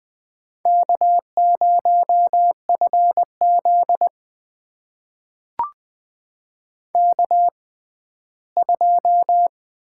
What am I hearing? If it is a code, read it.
Morse code, heard as a keyed tone, about 20 wpm, spaced as for 14 wpm, 700 Hz. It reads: K0FZ K2
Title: USA Call Signs - 4 Characters - 6x Character Spacing - 20wpm